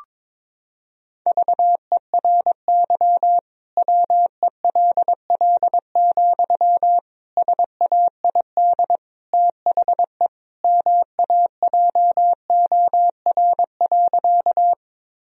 Morse code, heard as a keyed tone, about 22 wpm, 700 Hz. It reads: VERY WELL, SAID THE MAJOR.
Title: Sentences from Top 1000 Words - Review Part 4 - 22wpm